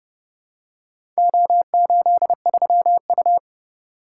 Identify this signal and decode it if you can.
Morse code, heard as a keyed tone, about 30 wpm, 700 Hz. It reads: O83U